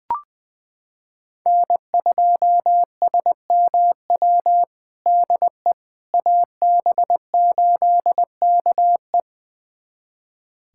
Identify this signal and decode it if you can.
Morse code, heard as a keyed tone, about 20 wpm, 700 Hz. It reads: N2SMW DE AB8KE
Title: USA Callsign DE Callsign - 5 Characters - 20wpm